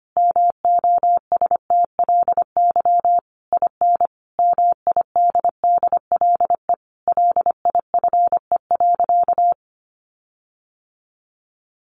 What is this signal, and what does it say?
Morse code, heard as a keyed tone, about 25 wpm, 700 Hz. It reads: MOSTLY IN MIDDLE LIFE.
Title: Sentences from Top 1300 Words 25wpm